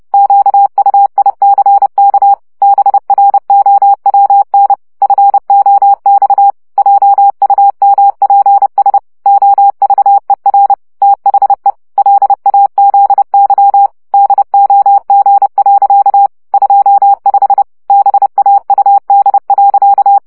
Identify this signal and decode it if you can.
Morse code, heard as a keyed tone, about 30 wpm, 800 Hz. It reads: QUICK BROWN FOX JUMPS OVER THE LAZY DOG. 25 BAUD.
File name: CW 25 baud